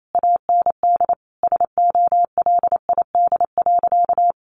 Morse code, heard as a keyed tone, about 28 wpm, 700 Hz. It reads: AND SOLID.
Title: Sentences from Top 1500 Words 28wpm